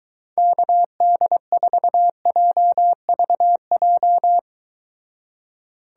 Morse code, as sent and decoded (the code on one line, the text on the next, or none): -.- -.. ....- .--- ...- .---
KD4JVJ